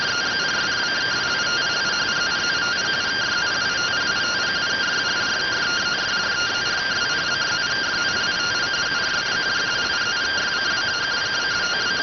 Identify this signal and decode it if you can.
Two-tone teleprinter signal, about 170 Hz shift.